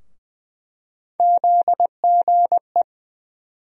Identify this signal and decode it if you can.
Morse code, heard as a keyed tone, about 21 wpm, 700 Hz. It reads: ZGE
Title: USA Call Sign Postfixes - 3 Letters 20wpm